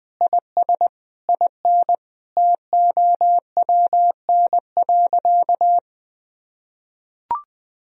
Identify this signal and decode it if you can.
Morse code, heard as a keyed tone, about 20 wpm, 700 Hz. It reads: IS IN TOWN.
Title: Sentences from Top 1000 Words - Review Part 2 - 20wpm